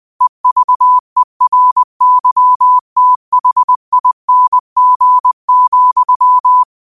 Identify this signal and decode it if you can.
Morse code, heard as a keyed tone, about 20 wpm, 1000 Hz. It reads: EVERYTHING,
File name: qotdfiles / 20wpm